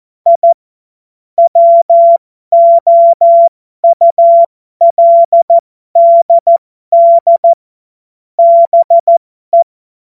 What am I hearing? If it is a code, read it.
Morse code, heard as a keyed tone, about 14 wpm, spaced as for 12 wpm, 680 Hz. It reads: I WOULDD BE